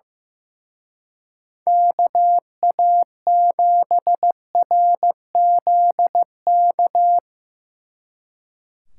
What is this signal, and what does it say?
Morse code, heard as a keyed tone, about 16 wpm, 700 Hz. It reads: KA7RZK
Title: Speed Racing - USA 2x3 Call Signs 15wpm